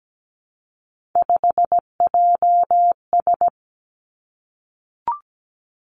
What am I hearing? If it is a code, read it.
Morse code, heard as a keyed tone, about 17 wpm, 700 Hz. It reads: 5JS
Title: Three Challenging Letter-Number Combinations 17wpm